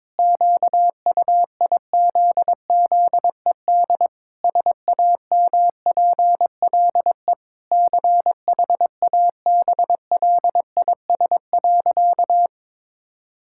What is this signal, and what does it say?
Morse code, heard as a keyed tone, about 22 wpm, 700 Hz. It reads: QUIZZED SAMPLE CHABLIS.
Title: Pangram Challenge v2 - 22wpm